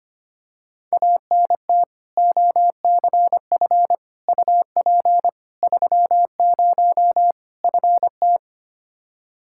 Morse code, heard as a keyed tone, about 25 wpm, 700 Hz. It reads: ANT OCF UP 30 FT
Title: Practice QSO - Exchange 2 - Info - 25wpm